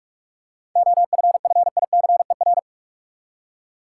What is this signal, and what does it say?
Morse code, heard as a keyed tone, about 43 wpm, 700 Hz. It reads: OFFICER
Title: Seven Letter Words - 45wpm